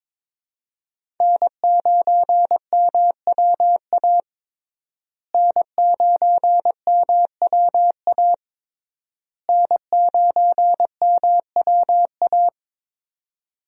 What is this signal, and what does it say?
Morse code, heard as a keyed tone, about 22 wpm, 700 Hz. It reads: N9MWA N9MWA N9MWA
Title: USA Call Signs - 5 Characters - Repeated 3x - 22wpm